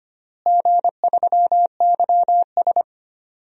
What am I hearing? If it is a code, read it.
Morse code, heard as a keyed tone, about 25 wpm, 700 Hz. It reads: G3YS
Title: DXCC-100 Call Signs - 4 Characters - Repeated 2x - 25wpm